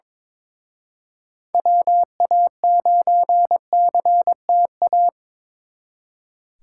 Morse code, heard as a keyed tone, about 26 wpm, 700 Hz. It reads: WA9CTA
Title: Speed Racing - USA 2x3 Call Signs 22wpm